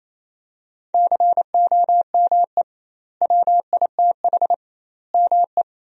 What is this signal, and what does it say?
Morse code, heard as a keyed tone, about 28 wpm, 700 Hz. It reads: COME WITH ME
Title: 2x Word-Spacing - Sentences from Top 200 Words - 28wpm